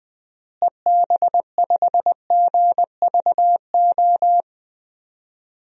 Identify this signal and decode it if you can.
Morse code, heard as a keyed tone, about 20 wpm, 700 Hz. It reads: EB5GVO
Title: DXCC-100 Call Signs - 6 Characters - Repeated 3x - 20wpm